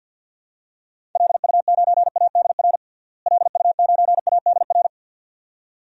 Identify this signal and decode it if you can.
Morse code, heard as a keyed tone, about 53 wpm, 700 Hz. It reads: LU9ADR LU9ADR
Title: DXCC-100 Call Signs - 6 Characters - Repeated 2x - 50wpm